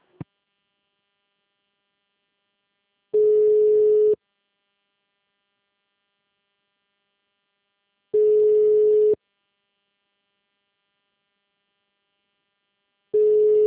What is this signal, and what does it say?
Ringback tone (Europe, 425 Hz).